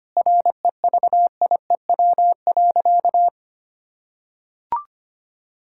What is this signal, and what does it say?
Morse code, heard as a keyed tone, about 25 wpm, 700 Hz. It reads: REVIEW.
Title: Sentences from Top 1000 Words - Review Part 6 - 25wpm